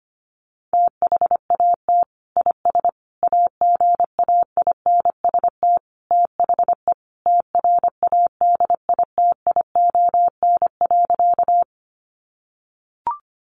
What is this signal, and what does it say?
Morse code, heard as a keyed tone, about 25 wpm, 700 Hz. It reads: THAT IS AGAINST THE TRADITION.
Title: Sentences from Top 1500 Words 25wpm